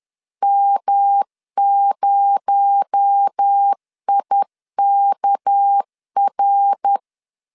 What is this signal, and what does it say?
Morse code, heard as a keyed tone, about 10 wpm, 800 Hz. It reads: M0IKR
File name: LEZIONI / 6 PKX0 / 6 PKX0 10wpm 2707